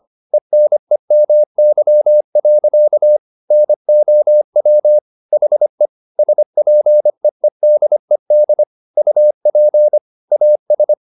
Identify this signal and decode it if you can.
Morse code, heard as a keyed tone, about 25 wpm, 600 Hz. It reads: ENEMY. NOW HE SPEEDED UP AS